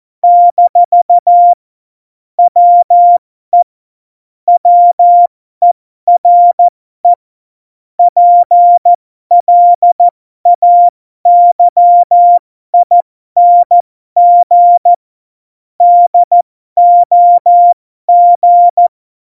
Morse code, heard as a keyed tone, about 14 wpm, spaced as for 12 wpm, 700 Hz. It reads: - WE WERE PLAYING DOG
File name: Morsekurs / Witze